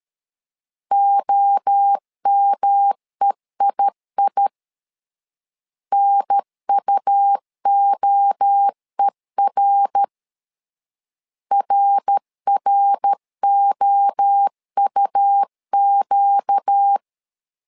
Morse code, heard as a keyed tone, about 13 wpm, 800 Hz. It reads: OMEII NUOER RROUQ
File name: LEZIONI / 4 Q 12wpm 1833